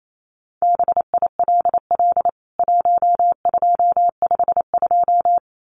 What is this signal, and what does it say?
Morse code, heard as a keyed tone, about 28 wpm, 700 Hz. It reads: BILL 1252
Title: CWT Exchange 28wpm